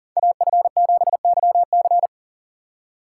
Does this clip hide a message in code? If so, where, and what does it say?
Yes: Morse code from 0.2 to 2.1 s, heard as a keyed tone, about 52 wpm, 700 Hz. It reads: AF7YC